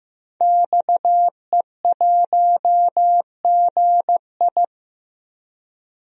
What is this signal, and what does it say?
Morse code, heard as a keyed tone, about 15 wpm, 700 Hz. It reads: XE1GI
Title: DXCC-100 Call Signs - 5 Characters - Repeated 2x - 15wpm